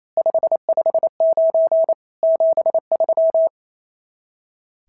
Morse code, heard as a keyed tone, about 28 wpm, 650 Hz. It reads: 559 73
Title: SOTA QSO Practice 28wpm